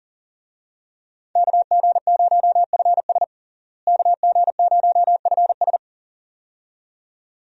Morse code, heard as a keyed tone, about 40 wpm, 700 Hz. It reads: KG0FS KG0FS
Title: USA Call Signs - 5 Characters - Repeated 2x - 40wpm